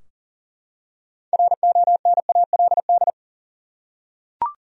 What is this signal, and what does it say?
Morse code, heard as a keyed tone, about 40 wpm, 700 Hz. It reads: RONALD
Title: USA Common Names - 40wpm